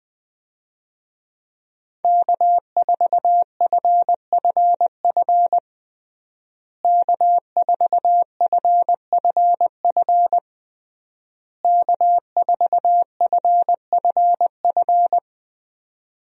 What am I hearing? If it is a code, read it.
Morse code, heard as a keyed tone, about 20 wpm, 700 Hz. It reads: K4FFF K4FFF K4FFF
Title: USA Call Signs - 5 Characters - Repeated 3x - 20wpm